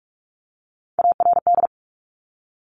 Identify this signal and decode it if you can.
Morse code, heard as a keyed tone, about 46 wpm, 700 Hz. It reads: ARD
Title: Top Three Letter Combinations in Words 45wpm